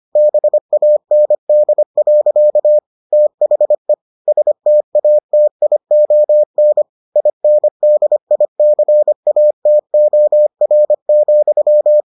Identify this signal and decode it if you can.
Morse code, heard as a keyed tone, about 25 wpm, 600 Hz. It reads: BAND. THE STATION INDICATOR,